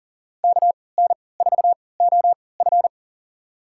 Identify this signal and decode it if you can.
Morse code, heard as a keyed tone, about 40 wpm, spaced as for 23 wpm, 700 Hz. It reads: KN4OF
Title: USA Call Signs - 3x Character Spacing - 40wpm